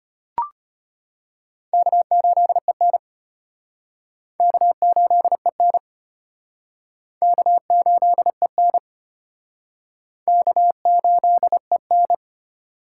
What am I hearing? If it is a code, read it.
Morse code, heard as a keyed tone, about 32 wpm, 700 Hz. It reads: K8EN K8EN K8EN K8EN